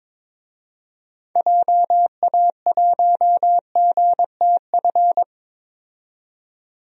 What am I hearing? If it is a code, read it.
Morse code, heard as a keyed tone, about 22 wpm, 700 Hz. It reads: JA1GTF
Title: DXCC-100 Call Signs - 6 Characters - 22wpm